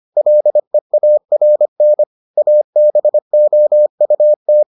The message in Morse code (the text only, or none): LEARN ABOUT